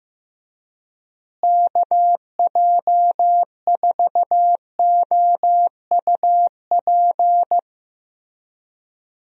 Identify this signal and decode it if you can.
Morse code, heard as a keyed tone, about 15 wpm, 700 Hz. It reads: KJ4OUP
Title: USA Call Signs - 6 Characters - Repeated 3x - 15wpm